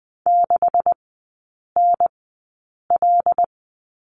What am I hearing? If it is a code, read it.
Morse code, heard as a keyed tone, about 20 wpm, spaced as for 8 wpm, 700 Hz. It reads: 6NL
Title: 178 Lesson 29e - Slash - Three Characters 20wpm